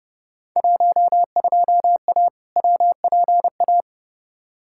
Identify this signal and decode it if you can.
Morse code, heard as a keyed tone, about 30 wpm, 700 Hz. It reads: 12A WPA
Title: Field Day Exchanges Encore - 30wpm